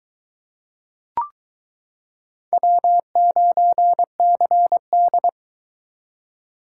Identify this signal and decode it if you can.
Morse code, heard as a keyed tone, about 23 wpm, 700 Hz. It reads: W9CD